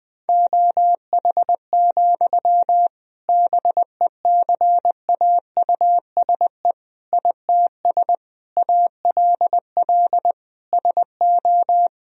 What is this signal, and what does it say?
Morse code, heard as a keyed tone, about 20 wpm, 700 Hz. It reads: OH, BECAUSE ITS ALL SO
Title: Sentences from Top 1000 Words - Review Part 3 - 20wpm